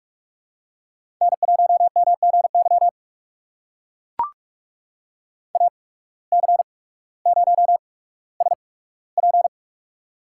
Morse code, heard as a keyed tone, about 42 wpm, spaced as for 19 wpm, 700 Hz. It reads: N1MGY AC0SP
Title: USA Call Signs - 8x Character Spacing - 45wpm